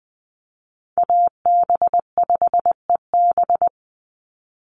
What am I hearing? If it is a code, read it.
Morse code, heard as a keyed tone, about 20 wpm, 700 Hz. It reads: AB5EB
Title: Call Signs Repeated 2x - 20wpm